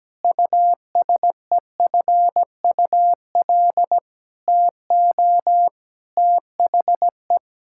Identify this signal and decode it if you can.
Morse code, heard as a keyed tone, about 17 wpm, 700 Hz. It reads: USEFUL TO THE
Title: Sentences from Top 1000 Words - Review Part 6 - 17wpm